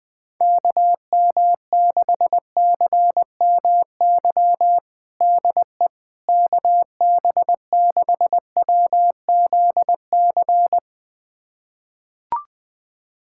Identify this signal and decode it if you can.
Morse code, heard as a keyed tone, about 20 wpm, 700 Hz. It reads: KM6CMY DE KB6WZC
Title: USA Callsign DE Callsign - 6 Characters - 20wpm